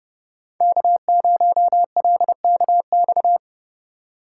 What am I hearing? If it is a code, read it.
Morse code, heard as a keyed tone, about 30 wpm, 700 Hz. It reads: K0LKX
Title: USA Call Signs - 5 Characters - 2x Character Spacing - 30wpm